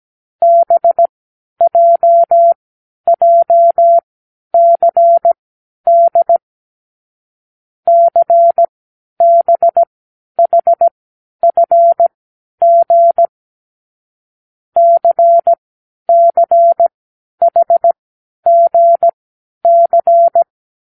Morse code, heard as a keyed tone, about 17 wpm, spaced as for 10 wpm, 680 Hz. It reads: BJJCD CBHFG CCHGC
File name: k7qo / Lesson15(A-J)